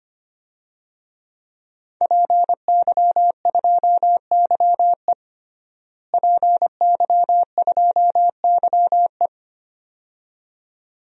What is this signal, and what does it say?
Morse code, heard as a keyed tone, about 25 wpm, 700 Hz. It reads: PY2YE PY2YE
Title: DXCC-100 Call Signs - 5 Characters - Repeated 2x - 25wpm